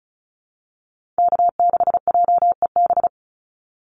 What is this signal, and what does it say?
Morse code, heard as a keyed tone, about 34 wpm, 700 Hz. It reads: K6JEB
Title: Call Signs Repeated 3x - 35wpm